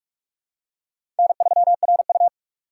Morse code, heard as a keyed tone, about 46 wpm, 700 Hz. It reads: N3RU
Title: USA Call Signs - 4 Characters - 2x Character Spacing - 45wpm